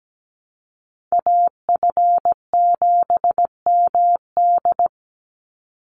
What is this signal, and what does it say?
Morse code, heard as a keyed tone, about 17 wpm, 700 Hz. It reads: AF7MD
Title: NA SOTA Chaser Callsigns 17wpm